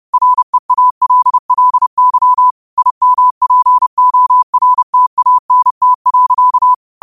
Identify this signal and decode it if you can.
Morse code, heard as a keyed tone, about 30 wpm, 1000 Hz. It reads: REALLY IMPORTANT.